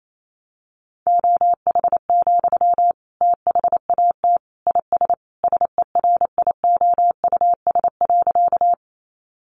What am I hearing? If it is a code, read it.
Morse code, heard as a keyed tone, about 28 wpm, 700 Hz. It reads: OH, THAT IS SERIOUS.
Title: Sentences from Top 1100 Words 28wpm